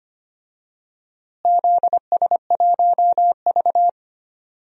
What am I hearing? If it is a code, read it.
Morse code, heard as a keyed tone, about 25 wpm, 700 Hz. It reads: ZS1V
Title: DXCC-100 Call Signs - 4 Characters - Repeated 2x - 25wpm